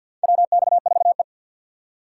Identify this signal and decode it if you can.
Morse code, heard as a keyed tone, about 52 wpm, 700 Hz. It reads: WX4E
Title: USA Call Signs - 4 Characters - 2x Character Spacing - 50wpm